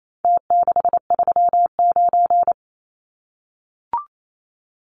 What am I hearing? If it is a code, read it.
Morse code, heard as a keyed tone, about 28 wpm, 700 Hz. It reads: T639
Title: Numbers - Contest Style - 4-Digit 28wpm